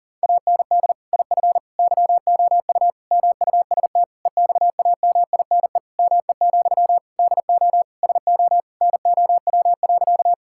Morse code, heard as a keyed tone, about 37 wpm, 700 Hz. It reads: AND IF YOU MUST EXAMINE ME, DO SO NOW.
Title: Sentences from Top 1600 Words 40wpm